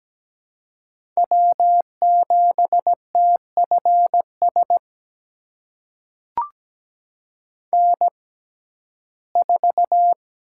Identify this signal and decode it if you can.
Morse code, heard as a keyed tone, about 17 wpm, spaced as for 12 wpm, 700 Hz. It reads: W7TFS N4
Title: USA Call Signs - 6x Character Spacing - 17wpm